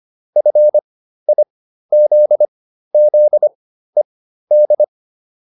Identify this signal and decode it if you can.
Morse code, heard as a keyed tone, about 25 wpm, spaced as for 13 wpm, 600 Hz. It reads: FIZZED